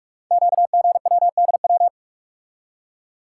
Morse code, heard as a keyed tone, about 44 wpm, 700 Hz. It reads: QGWDW
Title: Code Groups - 5 Letters - 45wpm